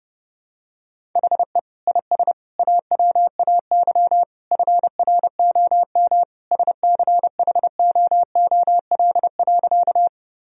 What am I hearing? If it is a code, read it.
Morse code, heard as a keyed tone, about 30 wpm, 700 Hz. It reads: HE IS AWAY FROM SCHOOL.
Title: Sentences from Top 1000 Words - Review Part 1 - 30wpm